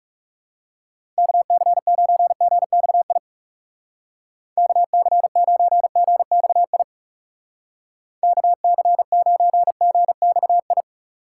Morse code, heard as a keyed tone, about 35 wpm, 700 Hz. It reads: KC9GXI KC9GXI KC9GXI
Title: Speed Racing - USA Call Signs 30wpm